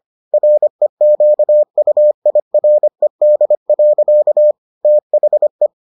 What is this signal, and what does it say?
Morse code, heard as a keyed tone, about 25 wpm, 600 Hz. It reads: REQUIRED. THE